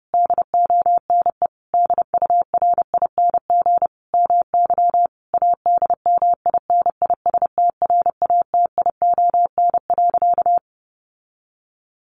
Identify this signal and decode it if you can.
Morse code, heard as a keyed tone, about 28 wpm, 700 Hz. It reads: DONE DURING MY ADMINISTRATION.
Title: Sentences from Top 1300 Words 30wpm